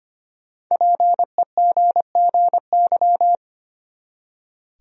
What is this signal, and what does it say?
Morse code, heard as a keyed tone, about 25 wpm, 700 Hz. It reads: PEGGY
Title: USA Common Names - Max 5 Letters - 25wpm